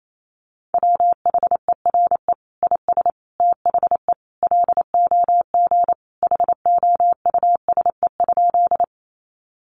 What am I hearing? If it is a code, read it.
Morse code, heard as a keyed tone, about 28 wpm, 700 Hz. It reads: WHERE IS THE LOG HOUSE?
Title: Sentences from Top 1400 Words 28wpm